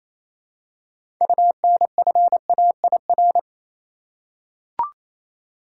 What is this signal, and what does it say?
Morse code, heard as a keyed tone, about 27 wpm, 700 Hz. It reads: UNFAIR
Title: Six Letter Words - 28wpm